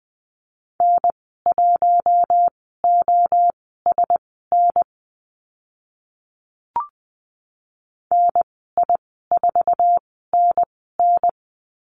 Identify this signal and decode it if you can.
Morse code, heard as a keyed tone, about 20 wpm, spaced as for 14 wpm, 700 Hz. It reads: N1OSN NI4NN
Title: 048 Lesson 9g - Number 4 - Call Signs 20wpm